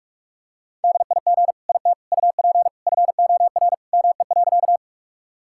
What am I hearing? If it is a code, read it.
Morse code, heard as a keyed tone, about 46 wpm, 700 Hz. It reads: DIG IT UP FOR ME.
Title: Sentences from Top 1600 Words 45wpm